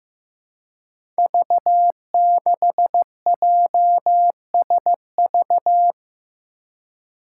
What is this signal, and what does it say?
Morse code, heard as a keyed tone, about 15 wpm, 700 Hz. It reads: V6JSV
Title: Five Challenging Letter-Number Combinations 15wpm